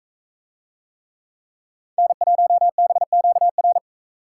Morse code, heard as a keyed tone, about 44 wpm, 700 Hz. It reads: N1BQR